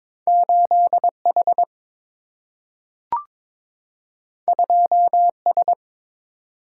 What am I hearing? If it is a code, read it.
Morse code, heard as a keyed tone, about 22 wpm, 700 Hz. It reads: 8 H 2 S
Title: Single-Letter Single-Number 22wpm